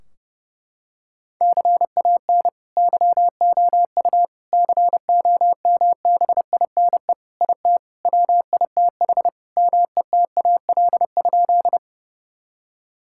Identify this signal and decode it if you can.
Morse code, heard as a keyed tone, about 30 wpm, 700 Hz. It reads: CAN YOU COMBINE IT WITH METAL?
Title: Sentences from Top 1600 Words 30wpm